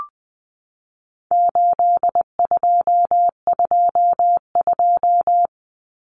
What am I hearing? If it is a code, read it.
Morse code, heard as a keyed tone, about 20 wpm, 700 Hz. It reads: 8222
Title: Common Cut Numbers - 4-Digit 20wpm